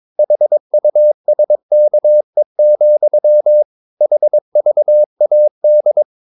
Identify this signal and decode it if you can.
Morse code, heard as a keyed tone, about 22 wpm, 600 Hz. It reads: HUSKE, HVAD
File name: Ferie QRS 22wpm